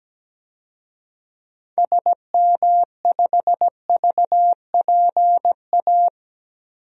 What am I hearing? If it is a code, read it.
Morse code, heard as a keyed tone, about 17 wpm, 700 Hz. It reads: SM5VPA
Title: DXCC-100 Call Signs - 6 Characters - 17wpm